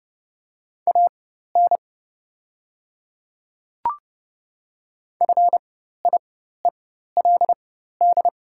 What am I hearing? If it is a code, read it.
Morse code, heard as a keyed tone, about 30 wpm, spaced as for 14 wpm, 690 Hz. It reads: AN FIELD
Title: 4x Character Spacing - Top 500 Words - 30wpm